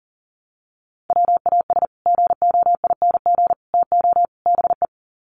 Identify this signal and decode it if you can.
Morse code, heard as a keyed tone, about 40 wpm, 700 Hz. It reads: WAS GOING TO BE
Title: Sets of 4 Words 40wpm